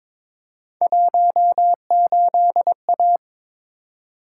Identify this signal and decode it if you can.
Morse code, heard as a keyed tone, about 22 wpm, 700 Hz. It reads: 18A
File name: practice / 22wpm / Three Letter-Number Combinations - 22wpm